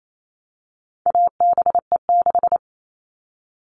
Morse code, heard as a keyed tone, about 28 wpm, 700 Hz. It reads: ABE6
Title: All Cut Numbers - 4-Digit 28wpm